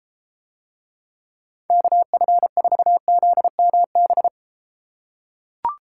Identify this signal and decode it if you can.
Morse code, heard as a keyed tone, about 34 wpm, 700 Hz. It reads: KF4ZMB